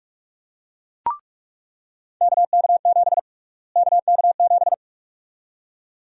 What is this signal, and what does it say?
Morse code, heard as a keyed tone, about 41 wpm, 700 Hz. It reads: KK7 KK7
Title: USA Call Sign Prefixes - Repeated 2x - 45wpm